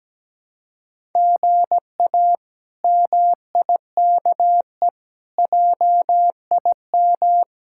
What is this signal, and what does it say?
Morse code, heard as a keyed tone, about 17 wpm, 700 Hz. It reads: GA MIKE JIM
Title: K1USN SST QSO Practice - 17wpm